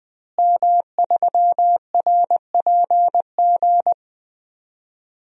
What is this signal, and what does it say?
Morse code, heard as a keyed tone, about 20 wpm, 700 Hz. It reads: M3RPG
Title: DXCC-100 Call Signs - 5 Characters - Repeated 2x - 20wpm